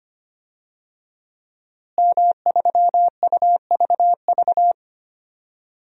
Morse code, heard as a keyed tone, about 25 wpm, 700 Hz. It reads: M3UVV
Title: DXCC-100 Call Signs - 5 Characters - Repeated 3x - 25wpm